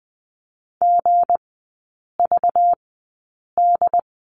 Morse code, heard as a keyed tone, about 20 wpm, spaced as for 8 wpm, 690 Hz. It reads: GVD
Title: 205 Lesson 33e - Letter j - Three Characters 20wpm